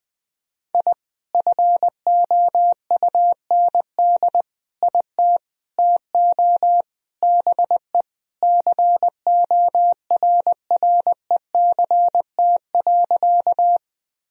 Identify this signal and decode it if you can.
Morse code, heard as a keyed tone, about 20 wpm, 700 Hz. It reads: I FOUND IT TO BE CORRECT.
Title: Sentences from Top 1000 Words - Review Part 2 - 20wpm